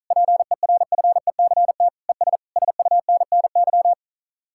Morse code, heard as a keyed tone, about 45 wpm, 700 Hz. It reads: PERFECT ES SUNNY